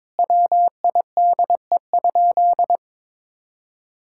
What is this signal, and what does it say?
Morse code, heard as a keyed tone, about 22 wpm, 700 Hz. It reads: WIDE?
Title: Sentences from Top 1000 Words - Review Part 6 - 22wpm